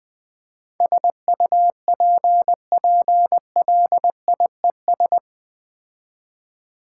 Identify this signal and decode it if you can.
Morse code, heard as a keyed tone, about 20 wpm, 700 Hz. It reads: SUPPLIES
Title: All-Age Appropriate Jokes - 20wpm